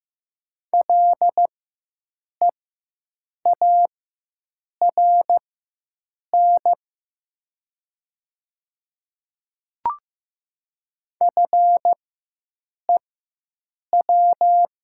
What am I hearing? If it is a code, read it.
Morse code, heard as a keyed tone, about 15 wpm, spaced as for 7 wpm, 700 Hz. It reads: LEARN FEW
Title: 4x Character Spacing - Top 300 Words - 15wpm